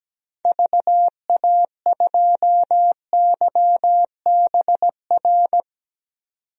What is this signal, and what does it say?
Morse code, heard as a keyed tone, about 17 wpm, 700 Hz. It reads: VA2YBR
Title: DXCC-100 Call Signs - 6 Characters - 17wpm